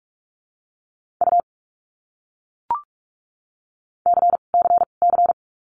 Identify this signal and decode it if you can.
Morse code, heard as a keyed tone, about 44 wpm, 700 Hz. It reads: U C C C
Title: Abbreviations Repeated 3x - 45wpm